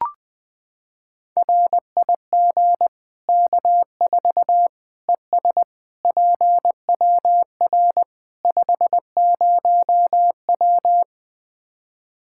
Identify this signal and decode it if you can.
Morse code, heard as a keyed tone, about 20 wpm, 700 Hz. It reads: RIG K4 ES PWR 50W
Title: Practice QSO - Exchange 2 - Info - 20wpm